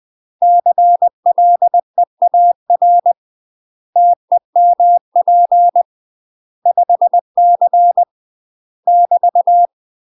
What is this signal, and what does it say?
Morse code, heard as a keyed tone, about 20 wpm, 700 Hz. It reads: CLEAR TEMP 5C <BT>
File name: UK MASTER QSOs TRAINING / 20WPM / UK MASTER QSOs 0014